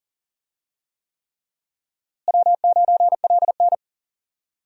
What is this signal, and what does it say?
Morse code, heard as a keyed tone, about 40 wpm, 700 Hz. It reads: W9LN
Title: USA Call Signs - 4 Characters - Repeated 3x - 40wpm